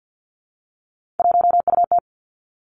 Morse code, heard as a keyed tone, about 50 wpm, 700 Hz. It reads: 1UT
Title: Three Challenging Letter-Number Combinations 50wpm